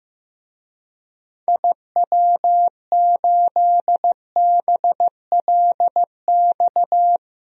Morse code, heard as a keyed tone, about 15 wpm, 700 Hz. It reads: IW8BLX
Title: DXCC-100 Call Signs - 6 Characters - Repeated 3x - 15wpm